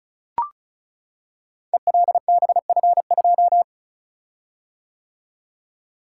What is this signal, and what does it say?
Morse code, heard as a keyed tone, about 34 wpm, 700 Hz. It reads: ELBF2
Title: Code Groups - 5 Letter:Numbers - 35wpm